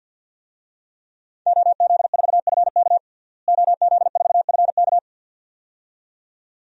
Morse code, heard as a keyed tone, about 52 wpm, 700 Hz. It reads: OZ4FK OZ4FK
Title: DXCC-100 Call Signs - 5 Characters - Repeated 2x - 50wpm